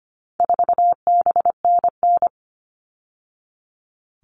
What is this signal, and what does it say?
Morse code, heard as a keyed tone, about 25 wpm, 700 Hz. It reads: 4BNN
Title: All Cut Numbers - 4-Digit 25wpm